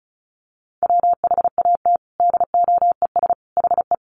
Morse code, heard as a keyed tone, about 34 wpm, 700 Hz. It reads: WHAT DOES HE
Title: Sentences from Top 1300 Words 35wpm